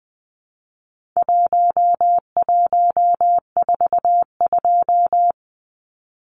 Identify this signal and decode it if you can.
Morse code, heard as a keyed tone, about 20 wpm, 700 Hz. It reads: 1142
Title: Numbers - Contest Style - 4-Digit 20wpm